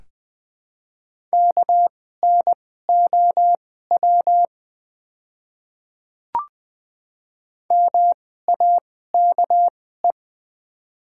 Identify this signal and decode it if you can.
Morse code, heard as a keyed tone, about 20 wpm, spaced as for 15 wpm, 700 Hz. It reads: KNOW MAKE
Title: 2x Character Spacing - Top 100 Words - 20wpm